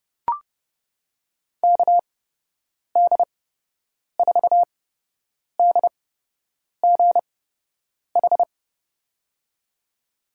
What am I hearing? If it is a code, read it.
Morse code, heard as a keyed tone, about 30 wpm, spaced as for 8 wpm, 700 Hz. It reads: KD4DGH
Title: USA Call Signs - 8x Character Spacing - 30wpm